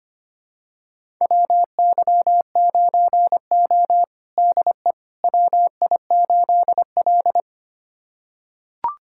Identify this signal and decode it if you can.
Morse code, heard as a keyed tone, about 25 wpm, 700 Hz. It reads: WY9O DE WI8L
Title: USA Callsign DE Callsign - 4 Characters - 25wpm